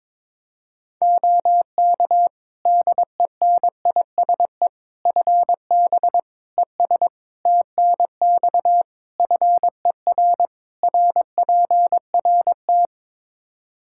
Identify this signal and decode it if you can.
Morse code, heard as a keyed tone, about 22 wpm, 700 Hz. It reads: OK DENISE FB ES TNX FER RPRT
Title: K1USN SST QSO Practice 22wpm